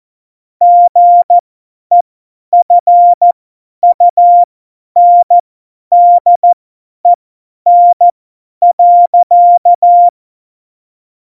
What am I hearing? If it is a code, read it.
Morse code, heard as a keyed tone, about 14 wpm, spaced as for 10 wpm, 700 Hz. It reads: GEFUNDEN.
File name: Morsekurs / Witze / Witz 05 Mai 2020, 50 F70